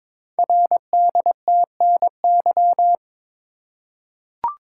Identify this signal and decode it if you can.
Morse code, heard as a keyed tone, about 22 wpm, 700 Hz. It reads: RDTNY
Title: Code Groups - 5 Letters - 22wpm